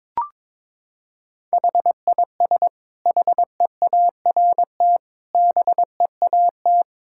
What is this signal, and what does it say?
Morse code, heard as a keyed tone, about 22 wpm, 700 Hz. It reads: HIS HEART BEAT
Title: Sentences from Top 1600 Words 22wpm